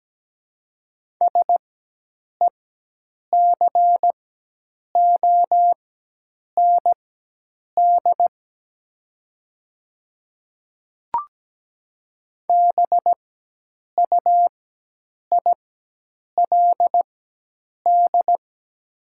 Morse code, heard as a keyed tone, about 17 wpm, spaced as for 8 wpm, 700 Hz. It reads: SECOND BUILD
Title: 4x Character Spacing - Top 300 Words - 17wpm